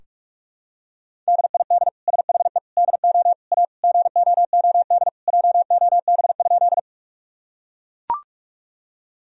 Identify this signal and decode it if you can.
Morse code, heard as a keyed tone, about 41 wpm, 700 Hz. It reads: DID SHE DO A GOOD JOB?
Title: Sentences from Top 1000 Words - Review Part 3 - 45wpm